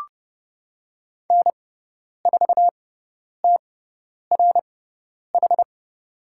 Morse code, heard as a keyed tone, about 30 wpm, spaced as for 10 wpm, 700 Hz. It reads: N4TRH
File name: Lesson 16g - Letter c - Call Signs 30wpm